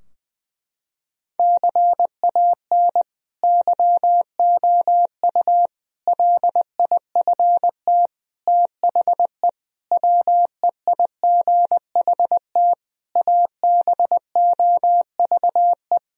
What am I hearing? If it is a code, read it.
Morse code, heard as a keyed tone, about 21 wpm, 700 Hz. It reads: CAN YOU LIFT THE WEIGHT ABOVE
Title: Sentences from Top 1000 Words - Review Part 3 - 20wpm